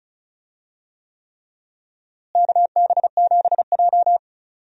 Morse code, heard as a keyed tone, about 36 wpm, 700 Hz. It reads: KB7J
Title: USA Call Signs - 4 Characters - Repeated 3x - 35wpm